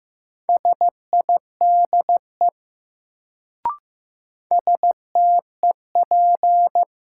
Morse code, heard as a keyed tone, about 15 wpm, 700 Hz. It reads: SIDE STEP
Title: Sentences from Top 400 Words - New Word Drill - 15wpm